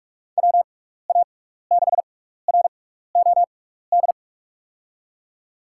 Morse code, heard as a keyed tone, about 43 wpm, spaced as for 15 wpm, 700 Hz. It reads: WA6ROD
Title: USA Call Signs - 6 Characters - 6x Character Spacing - 45wpm